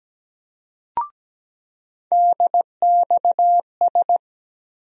Morse code, heard as a keyed tone, about 17 wpm, 700 Hz. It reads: DXS
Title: USA Call Sign Postfixes - 3 Letters 17wpm